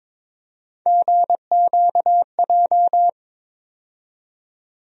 Morse code, heard as a keyed tone, about 22 wpm, 700 Hz. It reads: GQJ
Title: USA Call Sign Postfixes - 3 Letters 22wpm